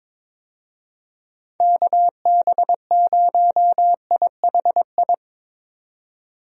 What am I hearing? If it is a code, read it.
Morse code, heard as a keyed tone, about 22 wpm, 700 Hz. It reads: KB0IHI
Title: USA Call Signs - 6 Characters - 22wpm